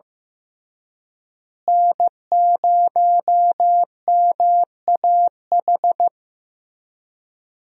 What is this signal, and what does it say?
Morse code, heard as a keyed tone, about 15 wpm, 700 Hz. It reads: N0MAH
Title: USA Call Signs - 5 Characters - Repeated 3x - 15wpm